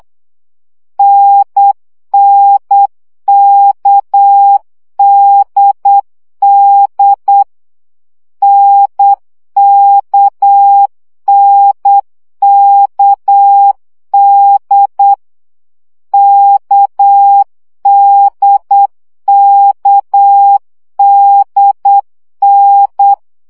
Morse code, heard as a keyed tone, about 8 wpm, 800 Hz. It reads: NNKDD NKNKD KDKDN
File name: CW gyakorlas n d k 7 baud